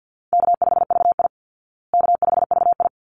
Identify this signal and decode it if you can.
Morse code, heard as a keyed tone, about 50 wpm, 700 Hz. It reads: K5VI K5VI
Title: Easy Call Signs Repeated 2x - 50wpm